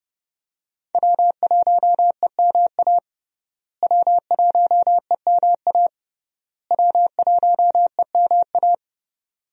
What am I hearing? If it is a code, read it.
Morse code, heard as a keyed tone, about 29 wpm, 700 Hz. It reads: W1EMA W1EMA W1EMA
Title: USA Call Signs - 5 Characters - Repeated 3x - 30wpm